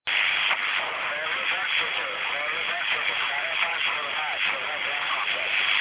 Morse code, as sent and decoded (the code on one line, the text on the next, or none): none